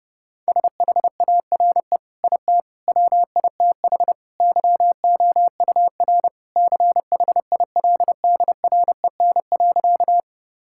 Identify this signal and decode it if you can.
Morse code, heard as a keyed tone, about 30 wpm, 700 Hz. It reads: SHARE IT WITH YOUR CHILDREN.
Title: Sentences from Top 1000 Words - Review Part 6 - 30wpm